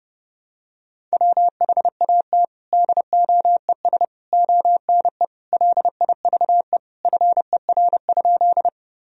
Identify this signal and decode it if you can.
Morse code, heard as a keyed tone, about 30 wpm, 700 Hz. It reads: WHAT DOES ONE LIVE FER?
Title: Sentences from Top 200 Words and Abbreviations 30wpm